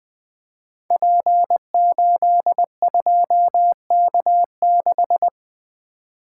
Morse code, heard as a keyed tone, about 20 wpm, 700 Hz. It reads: P82K6
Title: Code Groups - 5 Letter:Numbers - 20wpm